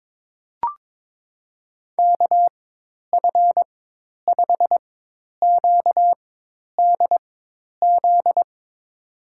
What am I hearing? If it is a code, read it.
Morse code, heard as a keyed tone, about 22 wpm, spaced as for 10 wpm, 700 Hz. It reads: KF5QDZ
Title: USA Call Signs - 6 Characters - 4x Character Spacing - 22wpm